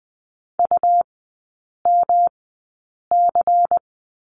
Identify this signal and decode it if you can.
Morse code, heard as a keyed tone, about 20 wpm, spaced as for 8 wpm, 700 Hz. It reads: UMC
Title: 107 Lesson 18e - Letter m - Three Characters 20wpm